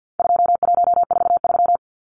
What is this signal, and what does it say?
Morse code, heard as a keyed tone, about 50 wpm, 700 Hz. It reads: 2143
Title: Numbers - 4-Digit 50wpm